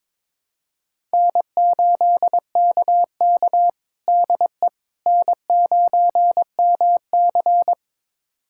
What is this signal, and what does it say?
Morse code, heard as a keyed tone, about 22 wpm, 700 Hz. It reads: N8KK DE N9MC
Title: USA Callsign DE Callsign - 4 Characters - 22wpm